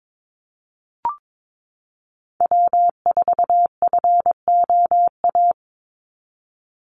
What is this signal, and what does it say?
Morse code, heard as a keyed tone, about 22 wpm, 700 Hz. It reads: W4FOA
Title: Call Signs Repeated 3x - 22wpm